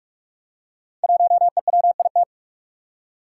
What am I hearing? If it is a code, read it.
Morse code, heard as a keyed tone, about 44 wpm, 700 Hz. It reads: 1EWIT
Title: Five Challenging Letter-Number Combinations 45wpm